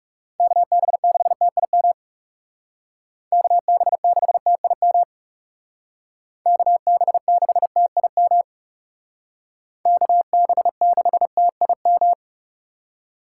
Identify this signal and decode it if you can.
Morse code, heard as a keyed tone, about 36 wpm, 700 Hz. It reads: KB6TIM KB6TIM KB6TIM KB6TIM
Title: Speed Racing - USA Call Signs 30wpm